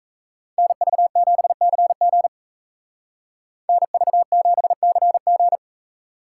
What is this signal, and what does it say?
Morse code, heard as a keyed tone, about 37 wpm, 700 Hz. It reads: NV7CG NV7CG